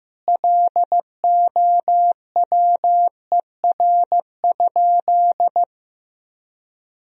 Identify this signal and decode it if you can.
Morse code, heard as a keyed tone, about 15 wpm, 700 Hz. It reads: LOWER?
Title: Sentences from Top 1000 Words - Review Part 4 - 15wpm